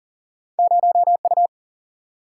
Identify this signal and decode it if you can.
Morse code, heard as a keyed tone, about 38 wpm, 700 Hz. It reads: 0U
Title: Single-Letter Single-Number 40wpm